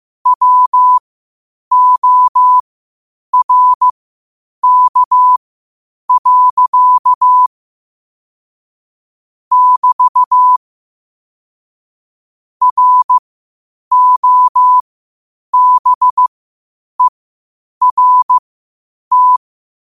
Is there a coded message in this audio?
Yes: Morse code from 0.3 to 19.4 s, heard as a keyed tone, about 15 wpm, spaced as for 8 wpm, 1000 Hz. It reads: WORK. <BT> ROBERT